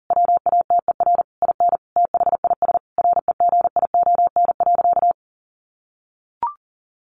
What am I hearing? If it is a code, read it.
Morse code, heard as a keyed tone, about 40 wpm, 700 Hz. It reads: WATER IN THIS REGION.
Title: Sentences from Top 1500 Words 40wpm